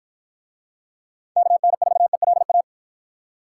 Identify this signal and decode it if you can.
Morse code, heard as a keyed tone, about 49 wpm, 700 Hz. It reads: KN4ELA